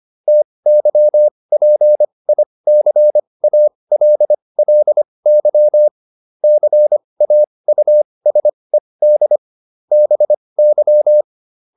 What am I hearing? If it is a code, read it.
Morse code, heard as a keyed tone, about 25 wpm, spaced as for 20 wpm, 600 Hz. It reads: TYPICALLY CAUSED BY